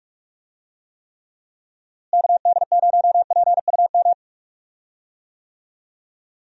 Morse code, heard as a keyed tone, about 41 wpm, 700 Hz. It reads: KD0PUM
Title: USA Call Signs - 6 Characters - Repeated 3x - 45wpm